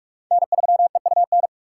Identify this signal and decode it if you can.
Morse code, heard as a keyed tone, about 44 wpm, 700 Hz. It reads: N3EUN
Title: USA Call Signs - 5 Characters - 3x Character Spacing - 45wpm